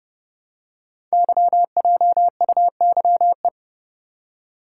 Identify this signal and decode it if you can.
Morse code, heard as a keyed tone, about 30 wpm, 690 Hz. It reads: YJUYE
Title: Code Groups - 5 Letters - 30wpm